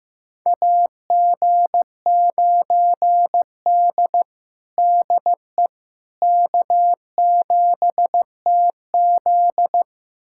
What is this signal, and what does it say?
Morse code, heard as a keyed tone, about 15 wpm, 700 Hz. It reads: AG9D DE K7TZ
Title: USA Callsign DE Callsign - 4 Characters - 15wpm